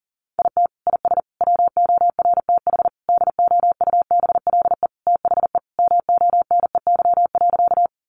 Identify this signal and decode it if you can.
Morse code, heard as a keyed tone, about 40 wpm, 700 Hz. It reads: IT IS WORTH DOUBLE THE MONEY.
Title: Sentences from Top 1400 Words 40wpm